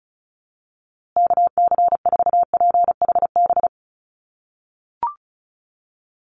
Morse code, heard as a keyed tone, about 34 wpm, 700 Hz. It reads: KC4PHB
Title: USA Call Signs Repeated 2x 35wpm